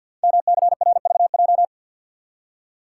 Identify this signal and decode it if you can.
Morse code, heard as a keyed tone, about 47 wpm, 700 Hz. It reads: MCRVJ
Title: Code Groups - 5 Letters - 50wpm